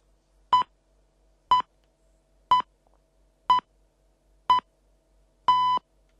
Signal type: radio time signal